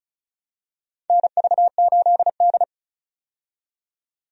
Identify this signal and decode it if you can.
Morse code, heard as a keyed tone, about 36 wpm, 690 Hz. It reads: NV8D
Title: USA Call Signs - 4 Characters - 35wpm